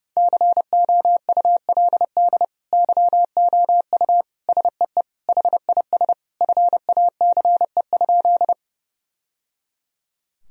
Morse code, heard as a keyed tone, about 30 wpm, 700 Hz. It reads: COULD YOU SEE HIS FACE?
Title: Sentences from Top 1000 Words - Review Part 2 - 30wpm